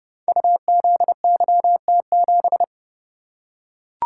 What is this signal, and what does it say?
Morse code, heard as a keyed tone, about 30 wpm, 700 Hz. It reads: UZYT7
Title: Code Groups - 5 Letter:Numbers - 30wpm